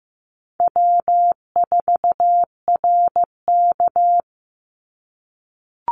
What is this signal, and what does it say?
Morse code, heard as a keyed tone, about 15 wpm, 700 Hz. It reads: W4RK
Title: NA SOTA Chaser Callsigns 15wpm